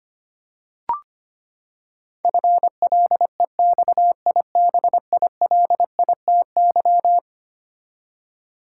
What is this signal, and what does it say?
Morse code, heard as a keyed tone, about 25 wpm, 700 Hz. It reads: FLEXIBILITY
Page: morse-code-ninja-practice-25wpm